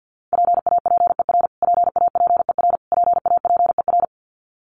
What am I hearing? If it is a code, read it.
Morse code, heard as a keyed tone, about 50 wpm, 700 Hz. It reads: PAPER PAPER PAPER
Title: Top 300 Words Repeated 50wpm